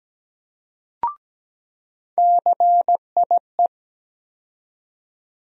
Morse code, heard as a keyed tone, about 17 wpm, 700 Hz. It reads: CIE
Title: USA Call Sign Postfixes - 3 Letters 17wpm